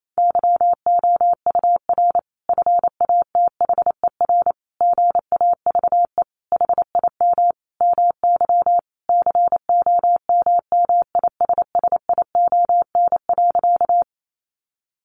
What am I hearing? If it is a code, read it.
Morse code, heard as a keyed tone, about 28 wpm, 700 Hz. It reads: YOUR FATHER GAVE HIM MY COMMISSION.
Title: Sentences from Top 1400 Words 28wpm